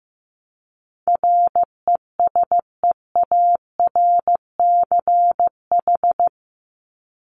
Morse code, heard as a keyed tone, about 15 wpm, 700 Hz. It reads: RESEARCH
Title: Top 501-750 Words 15wpm